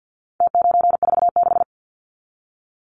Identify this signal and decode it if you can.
Morse code, heard as a keyed tone, about 50 wpm, 700 Hz. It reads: T946
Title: Numbers - Contest Style - 4-Digit 50wpm